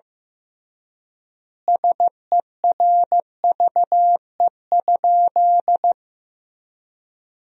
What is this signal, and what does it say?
Morse code, heard as a keyed tone, about 16 wpm, 700 Hz. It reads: SERVE?
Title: 3x Word-Spacing - Sentences from Top 400 Words - 15wpm